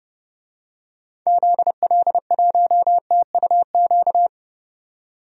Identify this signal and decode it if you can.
Morse code, heard as a keyed tone, about 30 wpm, 700 Hz. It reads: ZL1TUQ
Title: DXCC-100 Call Signs - 6 Characters - 30wpm